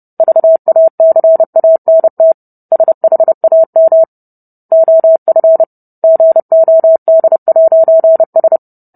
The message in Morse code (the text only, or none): VACANT SHAM OF GOD'S